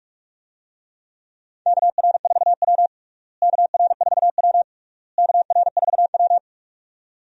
Morse code, heard as a keyed tone, about 46 wpm, 700 Hz. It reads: KR4W KR4W KR4W
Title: USA Call Signs - 4 Characters - Repeated 3x - 45wpm